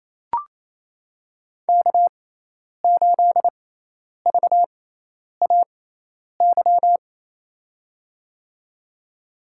Morse code, heard as a keyed tone, about 28 wpm, spaced as for 10 wpm, 700 Hz. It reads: K8VAY
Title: USA Call Signs - 6x Character Spacing - 28wpm